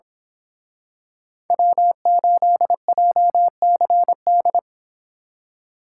Morse code, heard as a keyed tone, about 26 wpm, 700 Hz. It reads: W8JCD
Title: Speed Racing - USA 1x3 Call Signs 22wpm